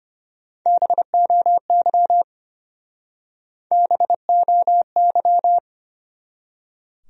Morse code, heard as a keyed tone, about 33 wpm, 700 Hz. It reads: BOY BOY